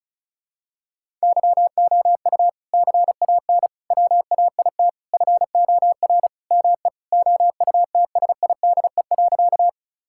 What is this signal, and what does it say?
Morse code, heard as a keyed tone, about 36 wpm, 700 Hz. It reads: YOU CAN WAIT FOR ME OUTSIDE.
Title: Sentences from Top 1000 Words - Review Part 6 - 35wpm